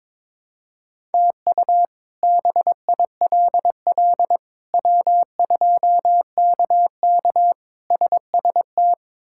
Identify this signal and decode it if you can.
Morse code, heard as a keyed tone, about 22 wpm, 700 Hz. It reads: TU BILL W2KK SST
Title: K1USN SST QSO Practice - 22wpm